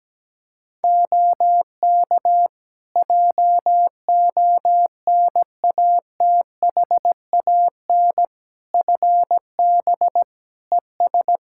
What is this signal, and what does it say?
Morse code, heard as a keyed tone, about 17 wpm, 700 Hz. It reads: OK JONATHAN FB ES
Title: Practice QSO - Exchange 2 - Info - 17wpm